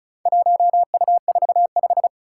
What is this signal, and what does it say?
Morse code, heard as a keyed tone, about 34 wpm, 700 Hz. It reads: 1U45
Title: Four Challenging Letter-Number Combinations 35wpm